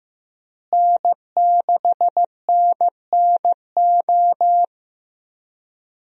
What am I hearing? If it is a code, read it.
Morse code, heard as a keyed tone, about 15 wpm, 700 Hz. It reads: N6NNO
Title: USA Call Signs - 5 Characters - Repeated 3x - 15wpm